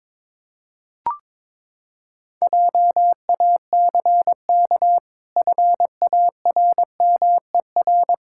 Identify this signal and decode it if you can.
Morse code, heard as a keyed tone, about 22 wpm, 700 Hz. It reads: JACK FARMER
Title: Pangram Challenge v2 - 22wpm